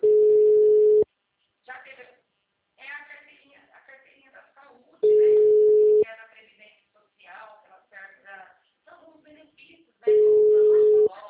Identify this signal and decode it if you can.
Ringback tone (Europe, 425 Hz), with speech.